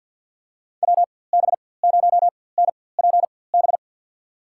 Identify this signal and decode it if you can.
Morse code, heard as a keyed tone, about 53 wpm, spaced as for 24 wpm, 700 Hz. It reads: WB0NPB
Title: USA Call Signs - 6 Characters - 4x Character Spacing - 50wpm